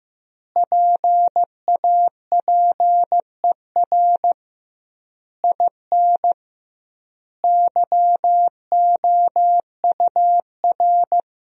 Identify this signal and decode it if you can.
Morse code, heard as a keyed tone, about 16 wpm, 700 Hz. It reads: PAPER IN YOUR
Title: 2x Word-Spacing - Sentences from Top 400 Words - 15wpm